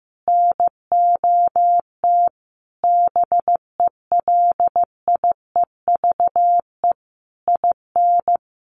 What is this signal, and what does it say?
Morse code, heard as a keyed tone, about 15 wpm, 700 Hz. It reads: NOT BELIEVE IN
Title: Sentences from Top 1500 Words 15wpm